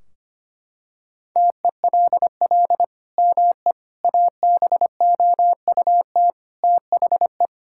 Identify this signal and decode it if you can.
Morse code, heard as a keyed tone, about 25 wpm, 700 Hz. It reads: TELL ME ABOUT THE
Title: Sentences from Top 1600 Words 25wpm